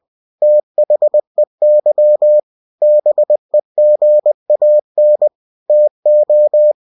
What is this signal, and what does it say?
Morse code, heard as a keyed tone, about 20 wpm, 600 Hz. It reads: THEY BEGAN TO
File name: cw-books / winnie-the-pooh / winnie-20wpm-page-0034